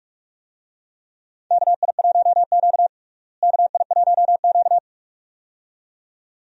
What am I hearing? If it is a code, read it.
Morse code, heard as a keyed tone, about 42 wpm, 700 Hz. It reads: KI1Q KI1Q
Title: USA Call Signs - 4 Characters - Repeated 2x - 45wpm